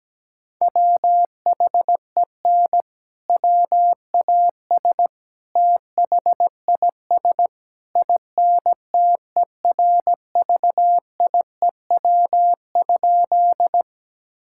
Morse code, heard as a keyed tone, about 17 wpm, 700 Hz. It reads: WHEN WAS THIS INTERVIEW?
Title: Sentences from Top 1600 Words 17wpm